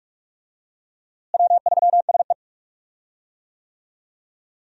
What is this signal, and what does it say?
Morse code, heard as a keyed tone, about 46 wpm, 700 Hz. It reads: W3SE
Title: USA Call Signs - 4 Characters - Repeated 2x - 45wpm